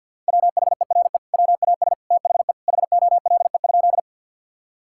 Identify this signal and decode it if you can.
Morse code, heard as a keyed tone, about 50 wpm, 700 Hz. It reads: WHERE WAS THE HOLE?
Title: Sentences from Top 1600 Words 50wpm